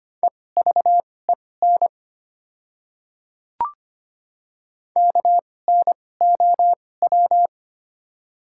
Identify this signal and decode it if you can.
Morse code, heard as a keyed tone, about 25 wpm, spaced as for 18 wpm, 700 Hz. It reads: EVEN KNOW
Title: 2x Character Spacing - Top 300 Words - 25wpm